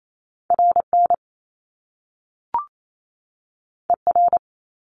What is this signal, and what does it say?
Morse code, heard as a keyed tone, about 28 wpm, 700 Hz. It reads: RN ER
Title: Top Two Letter Combinations in Words 28wpm